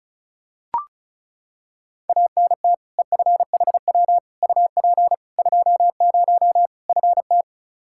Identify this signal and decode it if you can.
Morse code, heard as a keyed tone, about 36 wpm, 700 Hz. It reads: ANT EFHW UP 20 FT
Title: Practice QSO - Exchange 2 - Info - 35wpm